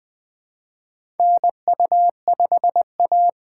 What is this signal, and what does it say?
Morse code, heard as a keyed tone, about 20 wpm, 700 Hz. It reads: NU5A
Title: K1USN SST QSO Practice - 20wpm